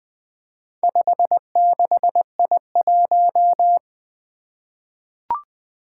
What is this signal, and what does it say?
Morse code, heard as a keyed tone, about 20 wpm, 700 Hz. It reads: 56I1
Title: Four Challenging Letter-Number Combinations 20wpm